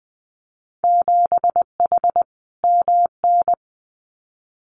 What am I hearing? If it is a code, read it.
Morse code, heard as a keyed tone, about 20 wpm, 700 Hz. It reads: 7H MN
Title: Winter Field Day Exchanges Encore 20wpm